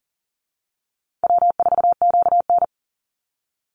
Morse code, heard as a keyed tone, about 40 wpm, 700 Hz. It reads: W4QN
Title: Call Signs - 40wpm